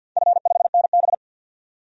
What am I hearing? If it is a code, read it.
Morse code, heard as a keyed tone, about 48 wpm, 700 Hz. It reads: F5NB
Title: DXCC-100 Call Signs - 4 Characters - 50wpm